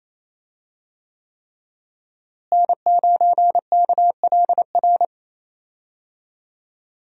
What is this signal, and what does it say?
Morse code, heard as a keyed tone, about 28 wpm, 700 Hz. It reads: N9KLR
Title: USA Call Signs - 5 Characters - Repeated 3x - 28wpm